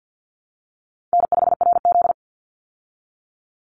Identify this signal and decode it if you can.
Morse code, heard as a keyed tone, about 49 wpm, 700 Hz. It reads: N5RZ
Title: Easy Call Signs Repeated 3x - 50wpm